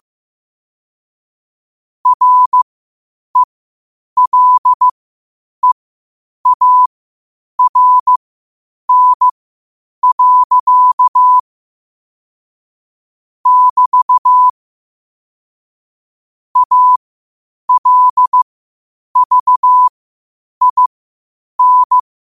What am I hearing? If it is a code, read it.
Morse code, heard as a keyed tone, about 15 wpm, spaced as for 8 wpm, 1000 Hz. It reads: RELEARN. <BT> ALVIN
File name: qotdfiles / 8wpm